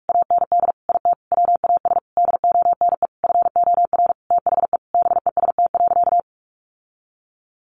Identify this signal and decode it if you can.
Morse code, heard as a keyed tone, about 44 wpm, 700 Hz. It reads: AND IT WAS DONE FOR THE BEST.
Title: Sentences from Top 600 Words 45wpm